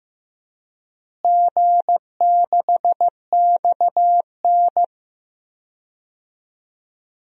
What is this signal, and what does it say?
Morse code, heard as a keyed tone, about 15 wpm, 700 Hz. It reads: G6XN
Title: DXCC-100 Call Signs - 4 Characters - Repeated 2x - 15wpm